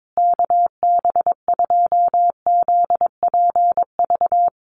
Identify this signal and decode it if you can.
Morse code, heard as a keyed tone, about 22 wpm, 700 Hz. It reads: KB2ZPV
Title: USA Call Signs Repeated 2x 22wpm